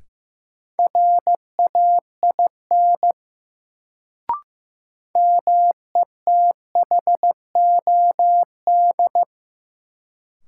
Morse code, heard as a keyed tone, about 15 wpm, 700 Hz. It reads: RAIN METHOD
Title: Sentences from Top 600 Words - New Word Drill - 15wpm